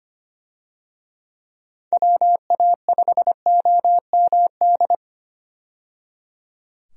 Morse code, heard as a keyed tone, about 25 wpm, 700 Hz. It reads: WA5OMD
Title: USA Call Signs - 6 Characters - 25wpm